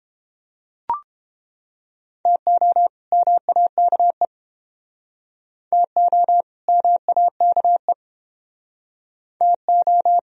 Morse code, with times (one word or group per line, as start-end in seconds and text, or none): 2.3-2.9 TO
3.1-4.3 MAKE
5.7-6.4 TO
6.7-7.9 MAKE
9.4-10.2 TO